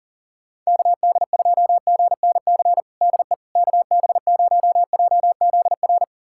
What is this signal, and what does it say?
Morse code, heard as a keyed tone, about 40 wpm, 700 Hz. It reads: KD2GNC DE KB0JZR
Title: USA Callsign DE Callsign - 6 Characters - 40wpm